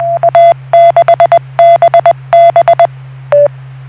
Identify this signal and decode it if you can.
Morse code, heard as a keyed tone, about 20 wpm, 680 Hz. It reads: A6BB